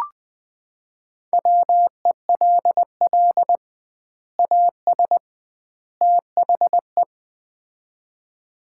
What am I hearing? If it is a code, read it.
Morse code, heard as a keyed tone, about 20 wpm, 700 Hz. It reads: WELL AS THE
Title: 2x Word-Spacing - Sets of 3 Words - 20wpm